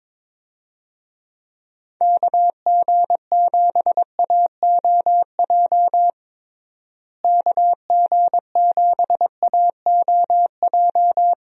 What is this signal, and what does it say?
Morse code, heard as a keyed tone, about 22 wpm, 700 Hz. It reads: KG7AOJ KG7AOJ
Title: USA Call Signs - 6 Characters - Repeated 3x - 22wpm